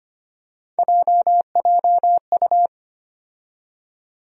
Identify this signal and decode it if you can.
Morse code, heard as a keyed tone, about 25 wpm, 700 Hz. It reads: JJU
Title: USA Call Sign Postfixes - 3 Letters 25wpm